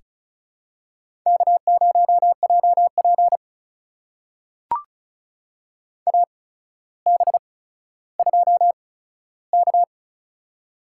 Morse code, heard as a keyed tone, about 35 wpm, spaced as for 15 wpm, 700 Hz. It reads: K0JP AB2K
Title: USA Call Signs - 4 Characters - 8x Character Spacing - 35wpm